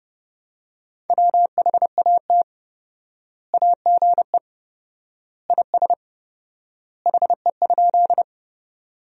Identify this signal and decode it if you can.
Morse code, heard as a keyed tone, about 30 wpm, 700 Hz. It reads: WHAT AGE IS HE?
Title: 4x Word-Spacing - Sentences from Top 500 Words - 30wpm